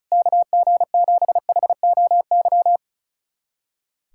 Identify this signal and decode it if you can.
Morse code, heard as a keyed tone, about 36 wpm, 700 Hz. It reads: KG7HOY
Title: USA Callsign DE Callsign 35wpm